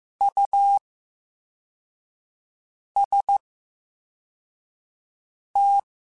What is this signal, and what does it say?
Morse code, heard as a keyed tone, about 15 wpm, spaced as for 4 wpm, 800 Hz. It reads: UST
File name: Special INGs / Special INGs-mp3s / Slow ing @13